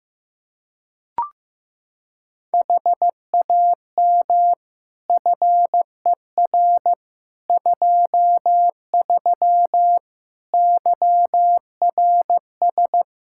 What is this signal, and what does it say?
Morse code, heard as a keyed tone, about 15 wpm, 700 Hz. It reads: HAM FER 23 YRS
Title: Practice QSO - Exchange 3 - More Info - 15wpm